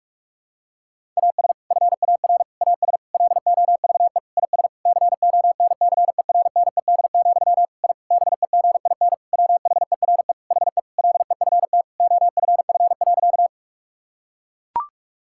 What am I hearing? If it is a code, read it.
Morse code, heard as a keyed tone, about 44 wpm, 700 Hz. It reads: AS FAR AS LOVE IS CONCERNED, I BEGIN WHERE HE LEFT OFF.
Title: Sentences from Top 1000 Words - Review Part 6 - 45wpm